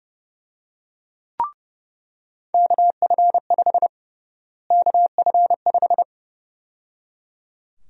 Morse code, heard as a keyed tone, about 32 wpm, 700 Hz. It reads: KF5 KF5
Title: USA Call Sign Prefixes - Repeated 2x - 30wpm